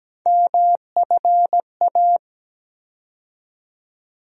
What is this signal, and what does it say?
Morse code, heard as a keyed tone, about 17 wpm, 700 Hz. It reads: MFA
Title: USA Call Sign Postfixes - 3 Letters 17wpm